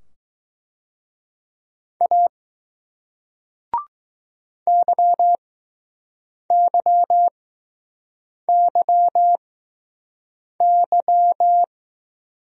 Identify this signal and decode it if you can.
Morse code, heard as a keyed tone, about 20 wpm, spaced as for 6 wpm, 700 Hz. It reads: A YYYY